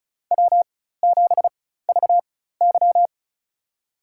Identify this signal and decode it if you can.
Morse code, heard as a keyed tone, about 36 wpm, spaced as for 16 wpm, 700 Hz. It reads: W7VY
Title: USA Call Signs - 4x Character Spacing - 35wpm